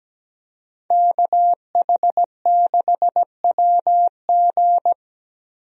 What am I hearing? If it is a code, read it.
Morse code, heard as a keyed tone, about 17 wpm, 700 Hz. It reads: KH6WG
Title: DXCC-100 Call Signs - 5 Characters - 17wpm